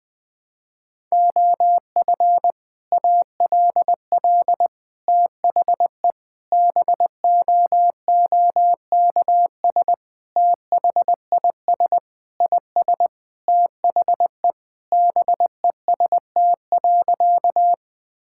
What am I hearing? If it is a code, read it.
Morse code, heard as a keyed tone, about 20 wpm, 700 Hz. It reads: OF ALL THE BOOKS THIS IS THE BEST.
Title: Sentences with 3-Word Phrases - 20wpm